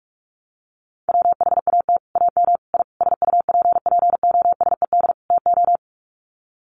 Morse code, heard as a keyed tone, about 44 wpm, 700 Hz. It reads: WHAT AM I SUPPOSED TO
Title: Sets of 5 Words Encore2 45wpm